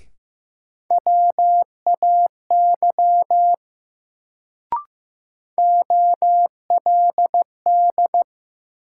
Morse code, heard as a keyed tone, about 15 wpm, 700 Hz. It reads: WAY OLD
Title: Sets of 2 Words - New Word Drill - 15wpm